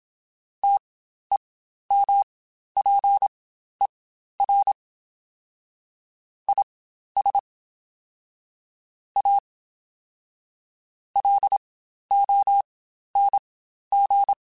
Morse code, heard as a keyed tone, about 27 wpm, spaced as for 12 wpm, 800 Hz. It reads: TEMPER IS A LONG
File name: Quotes Spaced / Quotes Spaced-mp3s / Temper sp @25